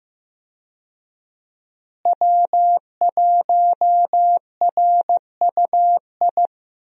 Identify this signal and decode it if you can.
Morse code, heard as a keyed tone, about 15 wpm, 700 Hz. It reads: W1RUI
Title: USA Call Signs - 5 Characters - Repeated 2x - 15wpm